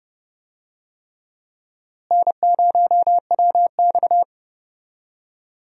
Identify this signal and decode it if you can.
Morse code, heard as a keyed tone, about 30 wpm, 700 Hz. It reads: N0WX
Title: USA Call Signs - 4 Characters - Repeated 3x - 30wpm